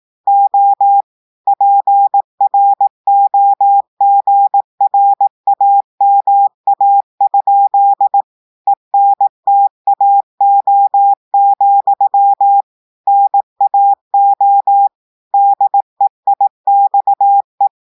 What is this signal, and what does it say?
Morse code, heard as a keyed tone, about 18 wpm, 800 Hz. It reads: O PROGRAMA? ENTAO, NAO DEIXE
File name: portugais texte en clair 2 vitesse 18 mots minutes